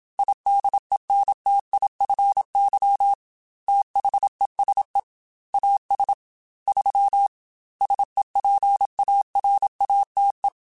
Morse code, heard as a keyed tone, about 27 wpm, 800 Hz. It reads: IDENTIFY THESE AS 3 SEPARATE
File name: Technical Subjects / Technical Subjects-mp3s / Sw Pwr Sup @25